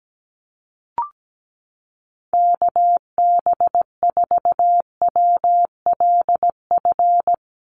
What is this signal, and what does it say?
Morse code, heard as a keyed tone, about 17 wpm, 700 Hz. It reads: KB4WLF
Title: Call Signs - 17wpm